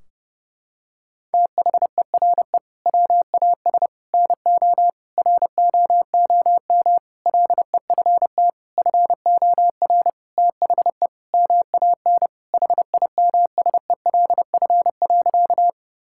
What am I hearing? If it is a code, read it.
Morse code, heard as a keyed tone, about 30 wpm, 700 Hz. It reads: THERE WAS NO ROOM LEFT FOR THE MAN HIMSELF.
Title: Sentences from Top 1600 Words 30wpm